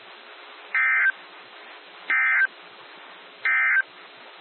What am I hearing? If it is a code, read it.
Emergency Alert System (SAME) tones, heard as harsh buzzing bursts. It is heard from 0.6 to 3.9 s.
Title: NOAA Weather Radio - WXJ73